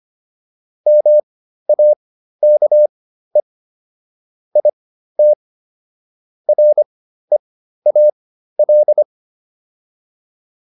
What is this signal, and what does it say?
Morse code, heard as a keyed tone, about 25 wpm, spaced as for 13 wpm, 600 Hz. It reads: MAKE IT REAL